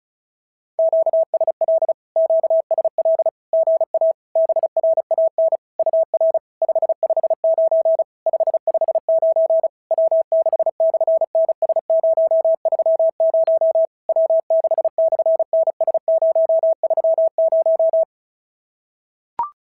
Morse code, heard as a keyed tone, about 36 wpm, 650 Hz. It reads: QSL QSL GA BRAN UR 559 559 W6/NS030 W6/NS030
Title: SOTA QSO Practice 35wpm